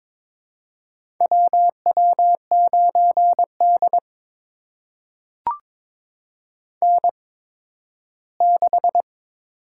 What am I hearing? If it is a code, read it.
Morse code, heard as a keyed tone, about 22 wpm, spaced as for 13 wpm, 700 Hz. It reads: WW9D N6
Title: USA Call Signs - 4 Characters - 8x Character Spacing - 22wpm